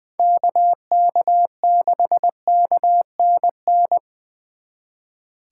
Morse code, heard as a keyed tone, about 24 wpm, 700 Hz. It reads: KK6KNN
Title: Speed Racing - USA 2x3 Call Signs 20wpm